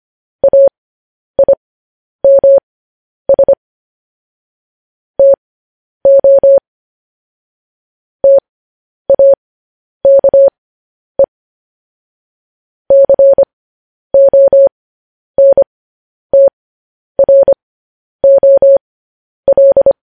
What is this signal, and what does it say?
Morse code, heard as a keyed tone, about 25 wpm, spaced as for 10 wpm, 550 Hz. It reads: AIMS TO TAKE CONTROL
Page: news headlines in morse code at 10 wpm